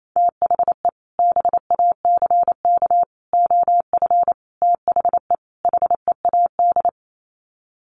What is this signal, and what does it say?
Morse code, heard as a keyed tone, about 28 wpm, 700 Hz. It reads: THE BACK OF THE HEAD
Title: Sets of 5 Words Encore2 28wpm